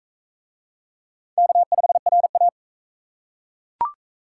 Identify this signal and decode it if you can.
Morse code, heard as a keyed tone, about 40 wpm, 700 Hz. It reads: K5RA